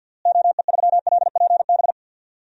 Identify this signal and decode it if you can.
Morse code, heard as a keyed tone, about 53 wpm, 700 Hz. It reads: OE3LPB
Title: DXCC-100 Call Signs - 6 Characters - Repeated 2x - 50wpm